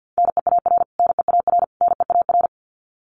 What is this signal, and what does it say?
Morse code, heard as a keyed tone, about 50 wpm, 700 Hz. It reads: NEAR NEAR NEAR
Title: Top 500 Words Repeated 50wpm